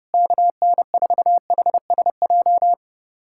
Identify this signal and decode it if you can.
Morse code, heard as a keyed tone, about 32 wpm, 700 Hz. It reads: KN4HSJ
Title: USA Callsign DE Callsign 30wpm